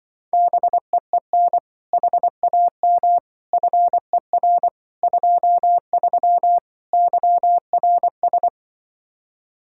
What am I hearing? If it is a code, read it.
Morse code, heard as a keyed tone, about 24 wpm, 700 Hz. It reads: BEEN HAM FER 23 YRS
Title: Speed-Racing Common QSO Phrases - 20wpm